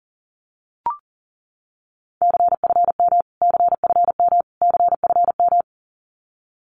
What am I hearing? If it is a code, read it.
Morse code, heard as a keyed tone, about 40 wpm, 700 Hz. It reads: CFM CFM CFM
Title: Abbreviations Repeated 3x - 40wpm